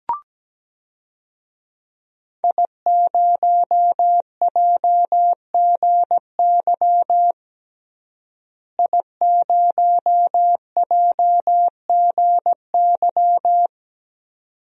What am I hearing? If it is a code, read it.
Morse code, heard as a keyed tone, about 17 wpm, 700 Hz. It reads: I0JGY I0JGY
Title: DXCC-100 Call Signs - 5 Characters - Repeated 2x - 17wpm